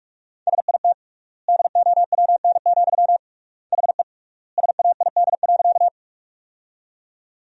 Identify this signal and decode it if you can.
Morse code, heard as a keyed tone, about 46 wpm, 700 Hz. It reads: SIT DOWN, HE SAID.
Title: 3x Word-Spacing - Sentences from Top 500 Words - 45wpm